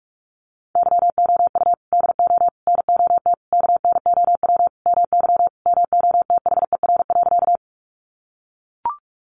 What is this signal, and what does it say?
Morse code, heard as a keyed tone, about 44 wpm, 700 Hz. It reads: YOU DO NOT KNOW MY MOTHER.
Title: Sentences from Top 600 Words 45wpm